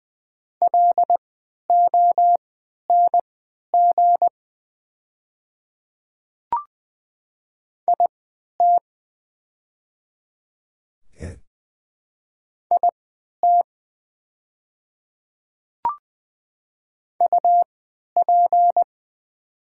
Morse code, heard as a keyed tone, about 20 wpm, spaced as for 11 wpm, 700 Hz. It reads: LONG IT IT UP
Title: 3x Character Spacing - Top 100 Words - 20wpm